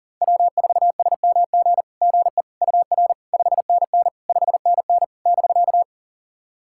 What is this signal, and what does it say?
Morse code, heard as a keyed tone, about 40 wpm, 700 Hz. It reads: W4SMG GE UR 5NN 5NN <BK>
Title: SOTA QSO Practice 40wpm